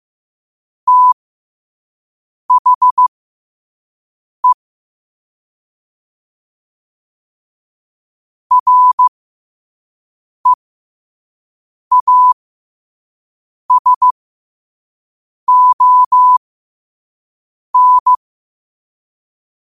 Morse code, heard as a keyed tone, about 15 wpm, spaced as for 5 wpm, 1000 Hz. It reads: THE REASON
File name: qotdfiles / 5wpm